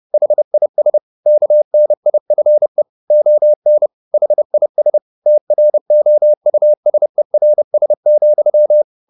Morse code, heard as a keyed tone, about 30 wpm, 600 Hz. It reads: HIS KNIFE ON HIS TROUSERS,
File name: the old man and the sea / the old man and the sea 1 30wpm